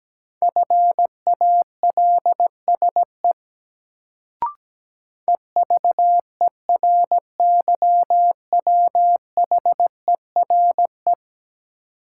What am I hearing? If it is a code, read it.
Morse code, heard as a keyed tone, about 17 wpm, 700 Hz. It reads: FALSE EVERYWHERE
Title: Sentences from Top 1600 Words - New Word Drill Alt - 17wpm